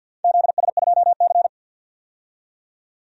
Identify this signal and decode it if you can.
Morse code, heard as a keyed tone, about 51 wpm, 700 Hz. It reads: ZS2C
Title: DXCC-100 Call Signs - 4 Characters - 50wpm